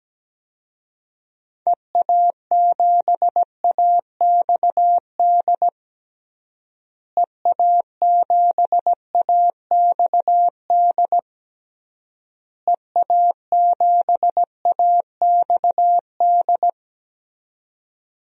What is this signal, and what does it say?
Morse code, heard as a keyed tone, about 17 wpm, 700 Hz. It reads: EA7AXD EA7AXD EA7AXD
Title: DXCC-100 Call Signs - 6 Characters - Repeated 3x - 17wpm